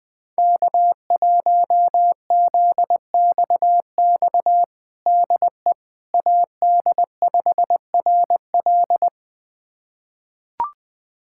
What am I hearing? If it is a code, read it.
Morse code, heard as a keyed tone, about 20 wpm, 700 Hz. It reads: K1ZXX DE AD5RL
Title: USA Callsign DE Callsign - 5 Characters - 20wpm